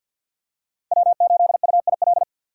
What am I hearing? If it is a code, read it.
Morse code, heard as a keyed tone, about 51 wpm, 700 Hz. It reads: W8UIL
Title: K1USN SST QSO Practice - 50wpm